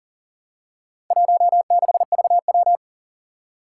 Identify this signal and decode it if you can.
Morse code, heard as a keyed tone, about 39 wpm, 700 Hz. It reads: 16VW
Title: Four Challenging Letter-Number Combinations 40wpm